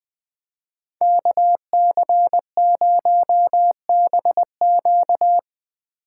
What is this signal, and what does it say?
Morse code, heard as a keyed tone, about 22 wpm, 700 Hz. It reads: KC0BQ
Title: Speed Racing - USA 2x2 Call Signs 20wpm